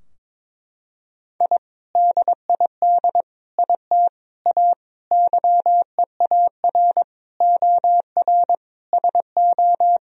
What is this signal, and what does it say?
Morse code, heard as a keyed tone, about 22 wpm, 700 Hz. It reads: I DID IT A YEAR OR SO
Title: Sentences from Top 1000 Words - Review Part 3 - 22wpm